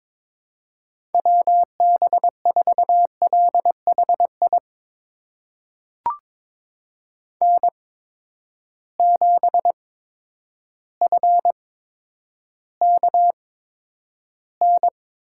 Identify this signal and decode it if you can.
Morse code, heard as a keyed tone, about 21 wpm, spaced as for 10 wpm, 700 Hz. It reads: WB4LHI N7FKN
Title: USA Call Signs - 8x Character Spacing - 22wpm